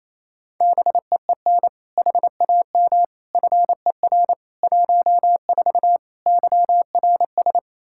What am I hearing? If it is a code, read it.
Morse code, heard as a keyed tone, about 28 wpm, 700 Hz. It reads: BEEN HAM FER 14 YRS
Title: Practice QSO - Exchange 3 - More Info - 28wpm